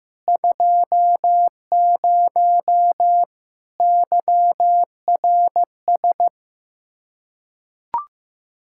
Morse code, heard as a keyed tone, about 15 wpm, 700 Hz. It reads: 20 YRS
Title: Practice QSO - Exchange 3 - More Info - 15wpm